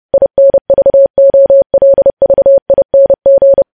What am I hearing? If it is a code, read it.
Morse code, heard as a keyed tone, about 30 wpm, 550 Hz. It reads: INVOLVING
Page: news headlines in morse code at 30 wpm